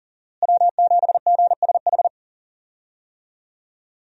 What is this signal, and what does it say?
Morse code, heard as a keyed tone, about 40 wpm, 700 Hz. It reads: W7GSH
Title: USA Call Signs - 5 Characters - Repeated 2x - 40wpm